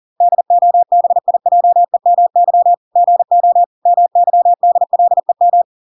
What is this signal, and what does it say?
Morse code, heard as a keyed tone, about 40 wpm, 700 Hz. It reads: DOBIJEMY GO MYDLEM